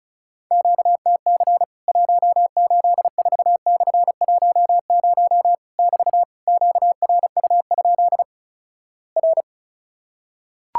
Morse code, heard as a keyed tone, about 36 wpm, 700 Hz. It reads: QTC 184/10 <BT> QRU? R
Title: Worked All Europe DX Contest - QTC Exchange - 35wpm